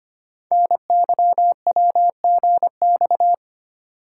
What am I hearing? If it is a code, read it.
Morse code, heard as a keyed tone, about 25 wpm, 700 Hz. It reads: NYWGX
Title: Code Groups - 5 Letter:Numbers - 25wpm